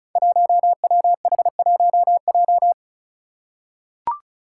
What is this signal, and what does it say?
Morse code, heard as a keyed tone, about 36 wpm, 700 Hz. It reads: 1WH1J
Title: Five Challenging Letter-Number Combinations 35wpm